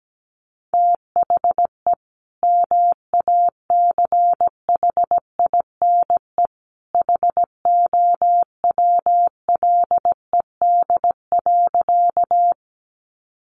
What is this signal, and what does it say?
Morse code, heard as a keyed tone, about 17 wpm, 700 Hz. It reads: THE MACHINE HOWLED.
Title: A Sound Of Thunder 17wpm